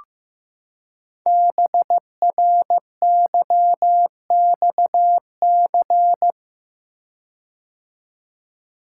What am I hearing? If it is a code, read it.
Morse code, heard as a keyed tone, about 15 wpm, 700 Hz. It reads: BRYXC
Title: Code Groups - 5 Letters - 15wpm